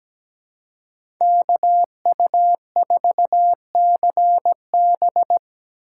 Morse code, heard as a keyed tone, about 17 wpm, 700 Hz. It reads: KU4CB
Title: USA Call Signs - 5 Characters - 4x Character Spacing - 17wpm